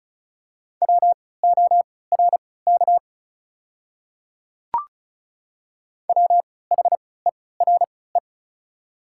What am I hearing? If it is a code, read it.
Morse code, heard as a keyed tone, about 36 wpm, spaced as for 20 wpm, 700 Hz. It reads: WORK WHERE
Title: 3x Character Spacing - Top 300 Words - 35wpm